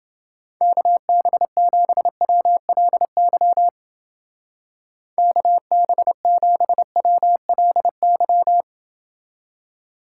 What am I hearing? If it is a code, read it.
Morse code, heard as a keyed tone, about 28 wpm, 700 Hz. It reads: KB7WLY KB7WLY